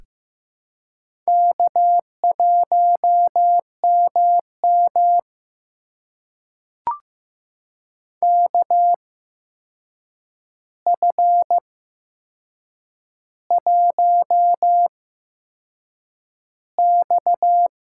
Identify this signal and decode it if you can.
Morse code, heard as a keyed tone, about 15 wpm, spaced as for 6 wpm, 700 Hz. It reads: K1MM KF1X
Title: USA Call Signs - 4 Characters - 8x Character Spacing - 15wpm